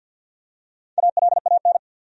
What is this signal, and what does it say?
Morse code, heard as a keyed tone, about 51 wpm, 700 Hz. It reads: ALAN
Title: USA Common Names - Max 4 Letters - 50wpm